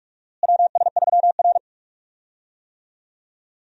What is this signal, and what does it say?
Morse code, heard as a keyed tone, about 41 wpm, 700 Hz. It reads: WS3R
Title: USA Call Signs - 4 Characters - Repeated 3x - 45wpm